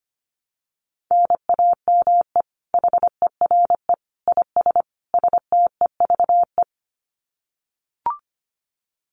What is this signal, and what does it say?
Morse code, heard as a keyed tone, about 25 wpm, 700 Hz. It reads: NAME HERE IS STEVE
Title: Common QSO Phrases 25wpm